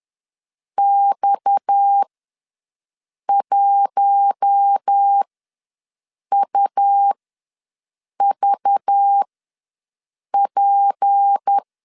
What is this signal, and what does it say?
Morse code, heard as a keyed tone, about 10 wpm, spaced as for 6 wpm, 800 Hz. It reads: X1UVP
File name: LEZIONI / 6 PKX0 / 6 PKX0 ws 10wpm 4220